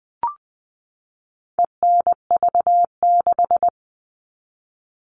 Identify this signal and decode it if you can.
Morse code, heard as a keyed tone, about 20 wpm, 700 Hz. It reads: ENV6
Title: All Cut Numbers - 4-Digit 20wpm